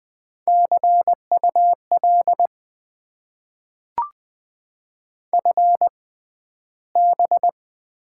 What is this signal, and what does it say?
Morse code, heard as a keyed tone, about 20 wpm, spaced as for 12 wpm, 700 Hz. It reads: CUL FB
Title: Top 90 QSO Elements - 6x Character Spacing - 20wpm